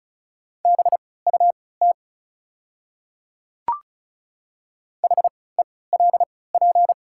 Morse code, heard as a keyed tone, about 36 wpm, spaced as for 20 wpm, 690 Hz. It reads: BUT HELP
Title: 3x Character Spacing - Top 200 Words - 35wpm